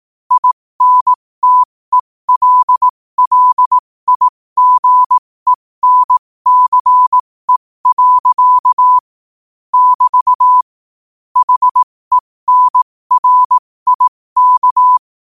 Morse code, heard as a keyed tone, about 18 wpm, spaced as for 15 wpm, 1000 Hz. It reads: INTELLIGENCE. <BT> HENRIK
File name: qotdfiles / 15wpm